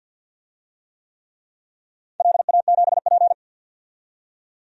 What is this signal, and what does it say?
Morse code, heard as a keyed tone, about 53 wpm, 700 Hz. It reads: LA7P
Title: DXCC-100 Call Signs - 4 Characters - 50wpm